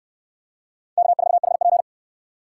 Morse code, heard as a keyed tone, about 67 wpm, 700 Hz. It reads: K4HP